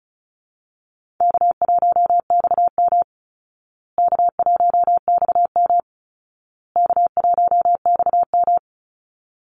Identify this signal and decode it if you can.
Morse code, heard as a keyed tone, about 34 wpm, 700 Hz. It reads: K1XM K1XM K1XM
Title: Easy Call Signs Repeated 3x - 35wpm